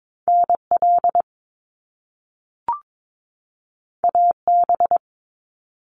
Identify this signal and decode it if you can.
Morse code, heard as a keyed tone, about 22 wpm, 700 Hz. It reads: NL AB
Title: Canadian Provinces and Territories 22wpm